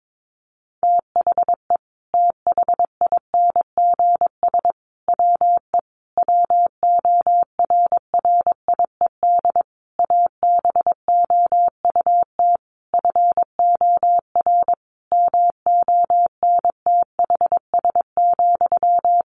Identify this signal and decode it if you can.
Morse code, heard as a keyed tone, about 22 wpm, 700 Hz. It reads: THE THINGS WE WORRIED ABOUT FOR MONTHS,
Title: A Sound Of Thunder 22wpm